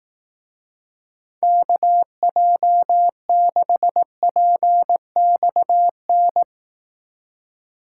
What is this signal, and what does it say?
Morse code, heard as a keyed tone, about 18 wpm, 700 Hz. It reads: KJ6PXN
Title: Speed Racing - USA 2x3 Call Signs 15wpm